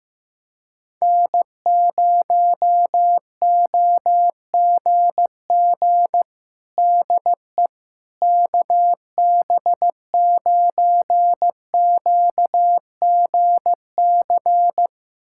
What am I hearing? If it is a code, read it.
Morse code, heard as a keyed tone, about 15 wpm, 700 Hz. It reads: N0OGG DE KB9QGC
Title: USA Callsign DE Callsign 15wpm